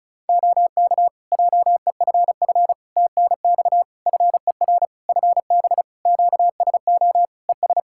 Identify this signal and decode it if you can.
Morse code, heard as a keyed tone, about 36 wpm, 700 Hz. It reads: OK JEFF TNX FER FB QSO ES
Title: K1USN SST QSO Practice 35wpm